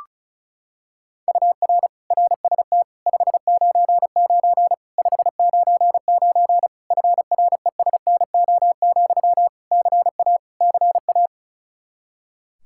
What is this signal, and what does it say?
Morse code, heard as a keyed tone, about 36 wpm, 700 Hz. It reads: UR RST 599 599 FRESNO, CA CA
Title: Practice QSO - Exchange 1 - Beginning - 35wpm